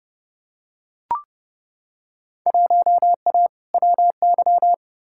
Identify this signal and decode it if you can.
Morse code, heard as a keyed tone, about 32 wpm, 700 Hz. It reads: 1A WY
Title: Field Day Exchanges Encore - 30wpm